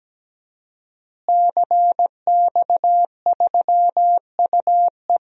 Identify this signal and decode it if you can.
Morse code, heard as a keyed tone, about 17 wpm, 700 Hz. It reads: CX3UE
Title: DXCC-100 Call Signs - 5 Characters - Repeated 3x - 17wpm